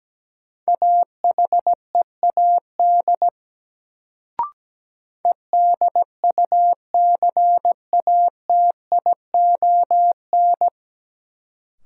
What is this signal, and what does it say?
Morse code, heard as a keyed tone, about 17 wpm, 700 Hz. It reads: AHEAD EDUCATION
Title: Sentences from Top 1000 Words - Review Part 6 - New Word Drill - 17wpm